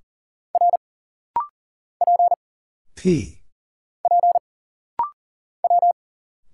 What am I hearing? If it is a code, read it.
Morse code, heard as a keyed tone, about 40 wpm, 700 Hz. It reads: R P P W
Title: Single Letters - Rapid-Fire 40wpm